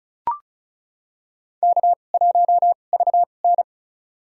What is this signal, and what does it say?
Morse code, heard as a keyed tone, about 36 wpm, spaced as for 25 wpm, 700 Hz. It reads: K1VN
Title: USA Call Signs - 2x Character Spacing - 35wpm